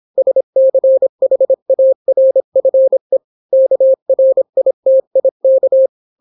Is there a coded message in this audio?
Yes: Morse code from 0.2 to 5.9 s, heard as a keyed tone, about 26 wpm, 530 Hz. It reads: SCHARFE KRITIK